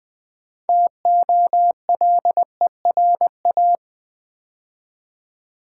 Morse code, heard as a keyed tone, about 20 wpm, 700 Hz. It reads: TOLERA
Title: Build A Word - 2 Letters At A Time - 8-Letter Words - 20wpm